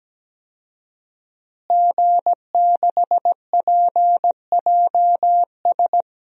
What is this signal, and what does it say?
Morse code, heard as a keyed tone, about 17 wpm, 700 Hz. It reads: G6PJS
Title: DXCC-100 Call Signs - 5 Characters - Repeated 2x - 17wpm